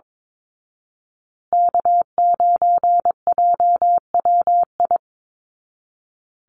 Morse code, heard as a keyed tone, about 22 wpm, 700 Hz. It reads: K9JWI
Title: Call Signs Repeated 2x - 22wpm